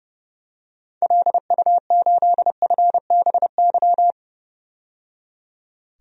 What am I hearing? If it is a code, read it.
Morse code, heard as a keyed tone, about 30 wpm, 700 Hz. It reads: LU8FBY
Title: DXCC-100 Call Signs - 6 Characters - 30wpm